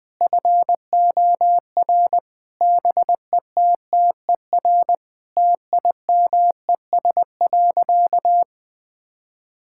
Morse code, heard as a keyed tone, about 21 wpm, 700 Hz. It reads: FOR BETTER TIMES.
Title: Sentences from Top 1000 Words - Review Part 6 - 20wpm